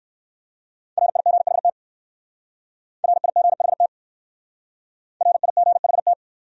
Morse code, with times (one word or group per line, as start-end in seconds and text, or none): 1.0-1.7 RIGHT
3.0-3.9 RIGHT
5.2-6.1 RIGHT